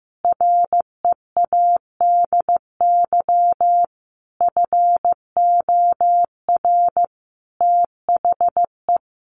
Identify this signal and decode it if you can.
Morse code, heard as a keyed tone, about 15 wpm, 700 Hz. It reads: READY FOR THE
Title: Sentences from Top 1500 Words 15wpm